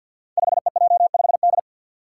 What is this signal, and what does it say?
Morse code, heard as a keyed tone, about 51 wpm, 700 Hz. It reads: 5EJ5D
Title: Five Challenging Letter-Number Combinations 50wpm